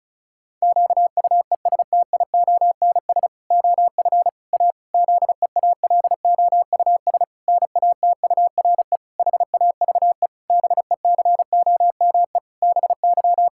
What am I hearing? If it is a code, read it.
Morse code, heard as a keyed tone, about 36 wpm, 700 Hz. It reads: QUESTIONS OF A ZEALOUS NATURE HAVE BECOME BY
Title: Pangram Challenge v2 - 35wpm